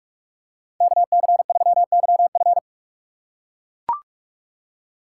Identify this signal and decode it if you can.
Morse code, heard as a keyed tone, about 44 wpm, 690 Hz. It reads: KC3YF
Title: USA Call Signs - 5 Characters - 8x Character Spacing - 45wpm